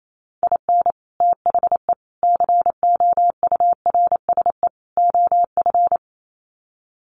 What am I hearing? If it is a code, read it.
Morse code, heard as a keyed tone, about 28 wpm, 700 Hz. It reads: IN THE COURSE OF
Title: Sets of 4 Words 28wpm